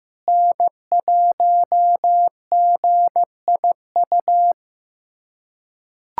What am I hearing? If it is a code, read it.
Morse code, heard as a keyed tone, about 15 wpm, 700 Hz. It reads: N1GIU
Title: USA Callsign DE Callsign - 5 Characters - 15wpm